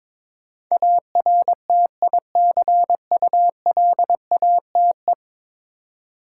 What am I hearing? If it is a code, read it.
Morse code, heard as a keyed tone, about 22 wpm, 700 Hz. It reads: ARTICULATE
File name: practice / 22wpm / 10 Letter Words - First 3 Letters Given - 22wpm